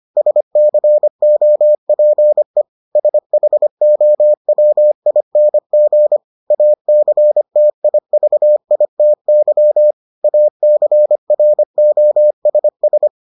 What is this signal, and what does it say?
Morse code, heard as a keyed tone, about 25 wpm, 600 Hz. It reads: SCOPE SHOWING ACTIVITY ACROSS